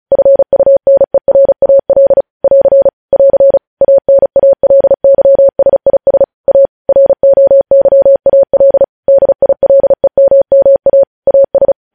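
Morse code, heard as a keyed tone, about 36 wpm, 550 Hz. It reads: FUNERAL <AR> <AR> ANALYSIS A ROYAL DILEMMA AS
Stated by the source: News headlines sent in Morse at speed 35.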